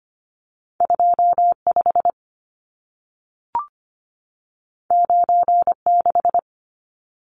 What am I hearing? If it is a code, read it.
Morse code, heard as a keyed tone, about 25 wpm, 700 Hz. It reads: 25 96
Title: Numbers - 2-Digit 25wpm